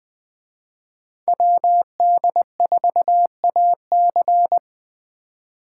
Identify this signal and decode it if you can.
Morse code, heard as a keyed tone, about 21 wpm, 700 Hz. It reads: WD4AC